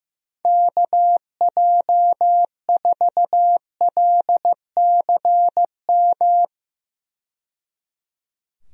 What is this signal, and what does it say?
Morse code, heard as a keyed tone, about 15 wpm, 700 Hz. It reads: KJ4LCM
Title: USA Call Signs - 6 Characters - 15wpm